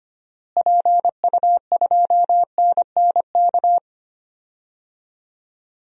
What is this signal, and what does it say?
Morse code, heard as a keyed tone, about 25 wpm, 700 Hz. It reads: PU2NNK
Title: DXCC-100 Call Signs - 6 Characters - Repeated 2x - 25wpm